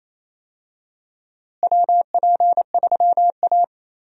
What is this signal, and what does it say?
Morse code, heard as a keyed tone, about 27 wpm, 700 Hz. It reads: WP3A
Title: USA Call Signs - 4 Characters - Repeated 2x - 28wpm